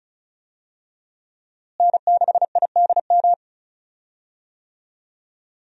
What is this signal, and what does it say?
Morse code, heard as a keyed tone, about 36 wpm, 700 Hz. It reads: N6IDM
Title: USA Call Signs - 5 Characters - Repeated 3x - 35wpm